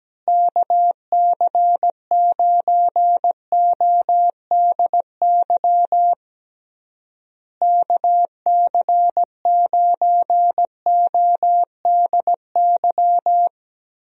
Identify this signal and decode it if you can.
Morse code, heard as a keyed tone, about 17 wpm, 700 Hz. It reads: KC9ODY KC9ODY
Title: USA Call Signs - 6 Characters - Repeated 3x - 17wpm